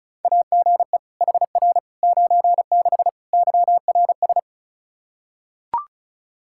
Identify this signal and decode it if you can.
Morse code, heard as a keyed tone, about 36 wpm, 700 Hz. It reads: AGE HR 96 YRS
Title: Practice QSO - Exchange 3 - More Info - 35wpm